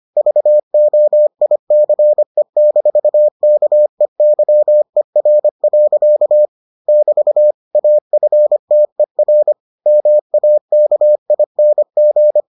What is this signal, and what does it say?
Morse code, heard as a keyed tone, about 25 wpm, 600 Hz. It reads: VOICE-KEYER. <BT> AFTER MAKING